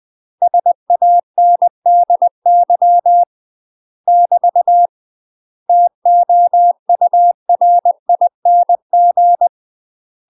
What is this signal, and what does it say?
Morse code, heard as a keyed tone, about 20 wpm, 700 Hz. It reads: SANDY <BT> TOURING
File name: UK MASTER QSOs TRAINING / 20WPM / UK MASTER QSOs 0050